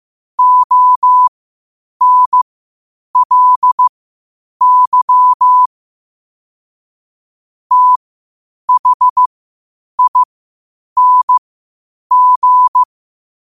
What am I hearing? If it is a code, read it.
Morse code, heard as a keyed tone, about 15 wpm, spaced as for 8 wpm, 1000 Hz. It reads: ONLY THING
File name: qotdfiles / 8wpm